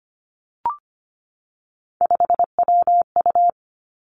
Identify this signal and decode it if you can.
Morse code, heard as a keyed tone, about 25 wpm, 700 Hz. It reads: 5WU
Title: Three Challenging Letter-Number Combinations 25wpm